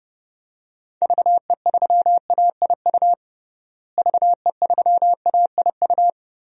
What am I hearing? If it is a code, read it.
Morse code, heard as a keyed tone, about 31 wpm, 700 Hz. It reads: VE3AIU VE3AIU
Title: DXCC-100 Call Signs - 6 Characters - Repeated 2x - 30wpm